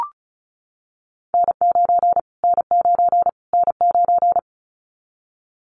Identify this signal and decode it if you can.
Morse code, heard as a keyed tone, about 34 wpm, 700 Hz. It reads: N9 N9 N9
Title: Callsign Prefixes for Top 25 DXCC - Repeated 3x - 35wpm